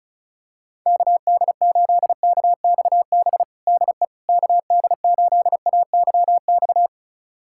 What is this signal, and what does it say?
Morse code, heard as a keyed tone, about 36 wpm, 700 Hz. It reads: KD8KXB DE KD8AYX
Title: USA Callsign DE Callsign 35wpm